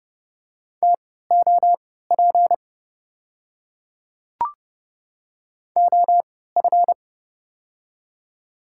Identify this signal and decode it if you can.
Morse code, heard as a keyed tone, about 30 wpm, spaced as for 17 wpm, 700 Hz. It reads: TOP OF
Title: 3x Character Spacing - Top 500 Words - 30wpm